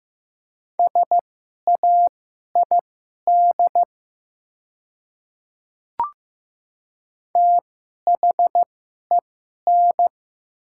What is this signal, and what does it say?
Morse code, heard as a keyed tone, about 15 wpm, spaced as for 11 wpm, 700 Hz. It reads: SAID THEN
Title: 2x Character Spacing - Top 100 Words - 15wpm